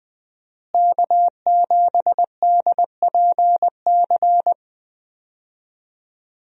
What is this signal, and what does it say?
Morse code, heard as a keyed tone, about 20 wpm, 700 Hz. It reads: K7DPC
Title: USA Call Signs - 5 Characters - Repeated 2x - 20wpm